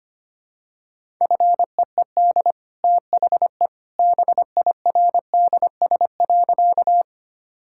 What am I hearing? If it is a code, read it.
Morse code, heard as a keyed tone, about 25 wpm, 700 Hz. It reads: FEED THE BIRDS.
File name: Easy Sentences with Hints - 25wpm